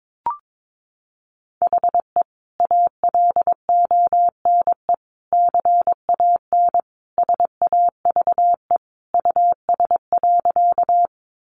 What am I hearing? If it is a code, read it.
Morse code, heard as a keyed tone, about 22 wpm, 700 Hz. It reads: HE ALONE CAN SAVE US.
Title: Sentences from Top 1100 Words 22wpm